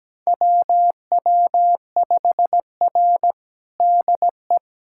Morse code, heard as a keyed tone, about 17 wpm, 700 Hz. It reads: WW5R DE
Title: USA Callsign DE Callsign - 4 Characters - 17wpm